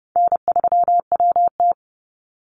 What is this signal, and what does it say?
Morse code, heard as a keyed tone, about 30 wpm, 700 Hz. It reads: N3WT
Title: Easy Call Signs - 30wpm